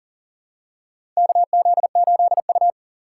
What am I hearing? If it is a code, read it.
Morse code, heard as a keyed tone, about 38 wpm, 700 Hz. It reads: KZ8U
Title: USA Call Signs - 4 Characters - 2x Character Spacing - 40wpm